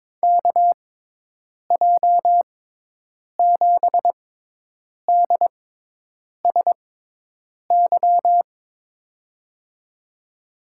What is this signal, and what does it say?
Morse code, heard as a keyed tone, about 21 wpm, spaced as for 8 wpm, 700 Hz. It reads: KJ7DSY
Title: USA Call Signs - 6x Character Spacing - 22wpm